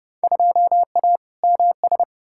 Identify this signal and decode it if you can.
Morse code, heard as a keyed tone, about 31 wpm, 700 Hz. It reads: 2A MS
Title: Field Day Exchanges - 30wpm